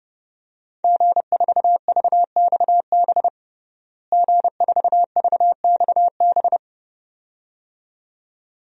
Morse code, heard as a keyed tone, about 32 wpm, 700 Hz. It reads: G4VXB G4VXB
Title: DXCC-100 Call Signs - 5 Characters - Repeated 2x - 30wpm